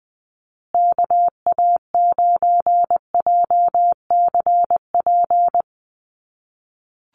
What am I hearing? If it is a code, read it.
Morse code, heard as a keyed tone, about 20 wpm, 700 Hz. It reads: KA9JCP
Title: USA Call Signs Repeated 2x 20wpm